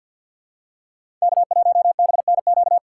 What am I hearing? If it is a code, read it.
Morse code, heard as a keyed tone, about 51 wpm, 700 Hz. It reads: K1BNQ
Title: USA Call Signs - 3x Character Spacing - 50wpm